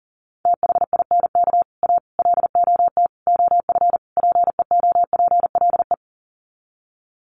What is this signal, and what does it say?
Morse code, heard as a keyed tone, about 40 wpm, 700 Hz. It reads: THINK A LOT OF PEOPLE
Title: Sets of 5 Words 40wpm